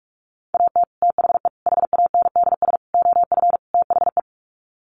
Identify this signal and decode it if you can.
Morse code, heard as a keyed tone, about 44 wpm, 700 Hz. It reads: AT THE HANDS OF THE
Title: Sets of 5 Words Encore2 45wpm